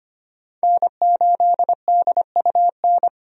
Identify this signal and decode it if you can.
Morse code, heard as a keyed tone, about 25 wpm, 700 Hz. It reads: N8DUN
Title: USA Call Signs - 5 Characters - 25wpm